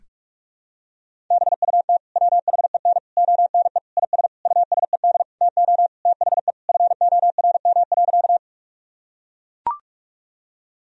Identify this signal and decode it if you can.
Morse code, heard as a keyed tone, about 41 wpm, 700 Hz. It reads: BUT WHEN ONE IS USED TO THE FORM.
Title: Sentences from Top 1000 Words - Review Part 1 - 45wpm